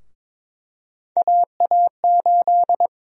Morse code, heard as a keyed tone, about 22 wpm, 700 Hz. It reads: AA8
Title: USA Call Sign Prefixes - 22wpm